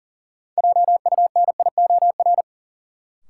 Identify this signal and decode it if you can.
Morse code, heard as a keyed tone, about 40 wpm, 700 Hz. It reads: JUNIOR
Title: Sentences from Top 1000 Words - Review Part 7 - New Word Drill - 40wpm